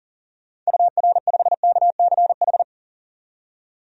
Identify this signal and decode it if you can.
Morse code, heard as a keyed tone, about 40 wpm, 700 Hz. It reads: UR5KCH
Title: DXCC-100 Call Signs - 6 Characters - 40wpm